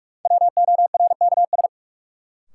Morse code, heard as a keyed tone, about 47 wpm, 700 Hz. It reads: WORKS
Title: Sentences from Top 1100 Words - New Word Drill Alt - 45wpm